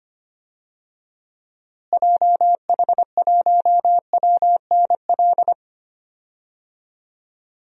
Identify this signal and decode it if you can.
Morse code, heard as a keyed tone, about 25 wpm, 700 Hz. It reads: JH1WNL
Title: DXCC-100 Call Signs - 6 Characters - Repeated 3x - 25wpm